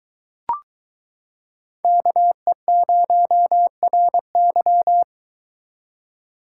Morse code, heard as a keyed tone, about 23 wpm, 700 Hz. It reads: KE0RY